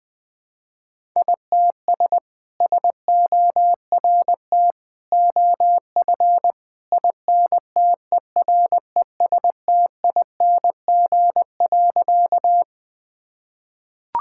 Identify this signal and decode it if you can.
Morse code, heard as a keyed tone, about 20 wpm, 700 Hz. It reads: ITS SORT OF INTERESTING.
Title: Sentences with 2-Word Phrases - 20wpm